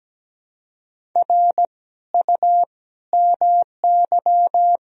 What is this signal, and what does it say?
Morse code, heard as a keyed tone, about 17 wpm, 700 Hz. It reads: R U MY
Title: Sentences from Top 300 Words and Abbreviations 17wpm